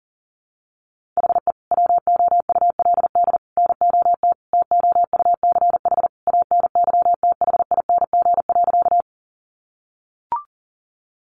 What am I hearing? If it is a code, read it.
Morse code, heard as a keyed tone, about 40 wpm, 700 Hz. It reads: HE WOULD NOT TOUCH ANYTHING.
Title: Sentences from Top 1200 Words 40wpm